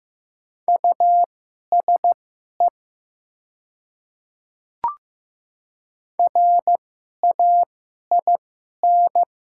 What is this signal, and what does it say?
Morse code, heard as a keyed tone, about 15 wpm, spaced as for 11 wpm, 700 Hz. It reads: USE RAIN
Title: 2x Character Spacing - Top 300 Words - 15wpm